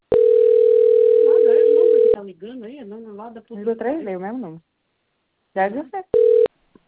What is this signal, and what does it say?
Ringback tone (North America), 440 + 480 Hz, with speech.
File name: out- - - - - .30